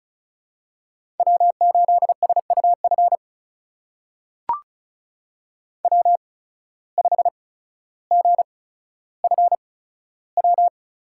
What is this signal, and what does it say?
Morse code, heard as a keyed tone, about 34 wpm, spaced as for 15 wpm, 700 Hz. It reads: W8SUF W5GFW
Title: USA Call Signs - 5 Characters - 8x Character Spacing - 35wpm